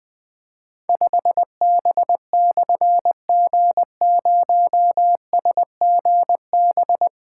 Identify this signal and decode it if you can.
Morse code, heard as a keyed tone, about 20 wpm, 700 Hz. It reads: 5B/G0SGB
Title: Difficult Call Signs 20wpm